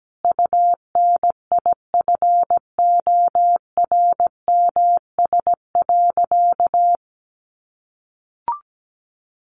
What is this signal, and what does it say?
Morse code, heard as a keyed tone, about 17 wpm, 700 Hz. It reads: UNIFORMS.
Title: Easy Phrases 17wpm